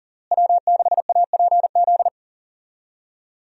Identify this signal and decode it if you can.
Morse code, heard as a keyed tone, about 40 wpm, 700 Hz. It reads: W6APZ
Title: USA Call Signs - 5 Characters - Repeated 3x - 40wpm